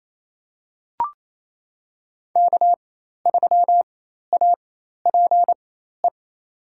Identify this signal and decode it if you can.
Morse code, heard as a keyed tone, about 28 wpm, spaced as for 13 wpm, 700 Hz. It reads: K3APE
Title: USA Call Signs - 4x Character Spacing - 28wpm